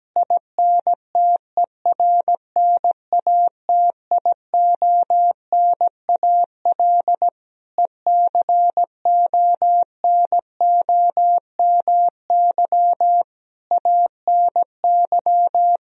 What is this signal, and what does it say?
Morse code, heard as a keyed tone, about 17 wpm, 700 Hz. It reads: INTERNATIONAL ECONOMY ANY
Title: Sentences from Top 1000 Words - Review Part 4 - 17wpm